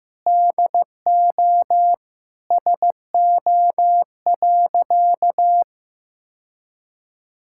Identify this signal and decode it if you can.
Morse code, heard as a keyed tone, about 15 wpm, 700 Hz. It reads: DO SO.
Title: Sentences from Top 1000 Words - Review Part 6 - 15wpm